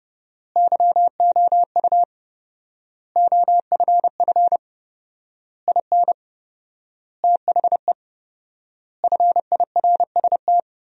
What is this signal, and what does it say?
Morse code, heard as a keyed tone, about 30 wpm, 700 Hz. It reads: YOU OFF IN THE FIRST
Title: 4x Word-Spacing - Sentences from Top 200 Words - 30wpm